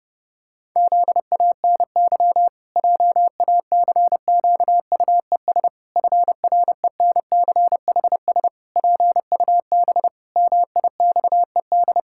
Morse code, heard as a keyed tone, about 30 wpm, 700 Hz. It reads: ZANY JACQUES FRENCHS PUB MIXED
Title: Pangram Challenge v2 - 30wpm